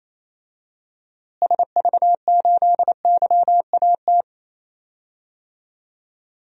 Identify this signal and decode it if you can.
Morse code, heard as a keyed tone, about 28 wpm, 700 Hz. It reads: SV8YAT
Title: DXCC-100 Call Signs - 6 Characters - Repeated 2x - 28wpm